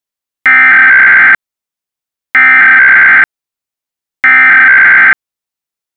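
Emergency Alert System (SAME) tones (header bursts), heard as harsh buzzing bursts. It is heard from 0.5 to 5.1 s.
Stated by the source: EAS SAME message generated by a Python encoder script.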